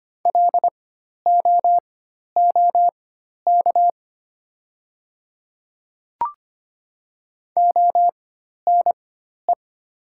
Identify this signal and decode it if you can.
Morse code, heard as a keyed tone, about 25 wpm, spaced as for 12 wpm, 700 Hz. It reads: LOOK ONE
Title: 4x Character Spacing - Top 100 Words - 25wpm